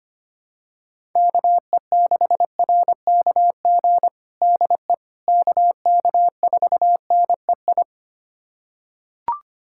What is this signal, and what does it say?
Morse code, heard as a keyed tone, about 25 wpm, 700 Hz. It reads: KE6RKG DE KK4NEI
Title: USA Callsign DE Callsign 25wpm